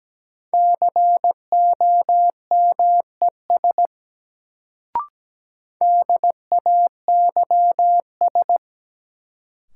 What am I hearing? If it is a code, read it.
Morse code, heard as a keyed tone, about 17 wpm, 700 Hz. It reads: COMES DAYS
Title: Sentences from Top 1000 Words - Review Part 5 - New Word Drill - 17wpm